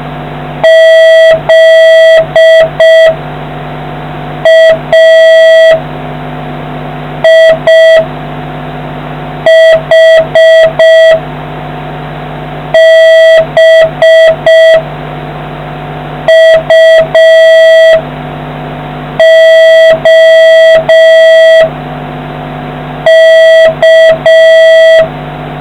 Morse code, heard as a keyed tone, about 6 wpm, 650 Hz. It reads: Z A I H B U O K